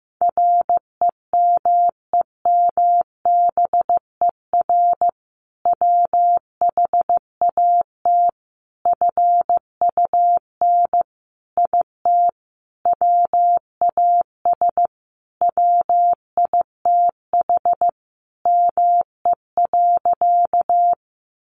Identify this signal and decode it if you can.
Morse code, heard as a keyed tone, about 15 wpm, 700 Hz. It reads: REMEMBER WHAT FUN IT WAS WITH ME.
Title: Sentences from Top 700 Words 15wpm